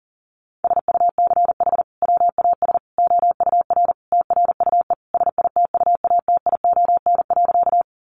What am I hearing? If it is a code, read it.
Morse code, heard as a keyed tone, about 40 wpm, 700 Hz. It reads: SUCH WAS OUR TRUE SITUATION.
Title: Sentences from Top 1200 Words 40wpm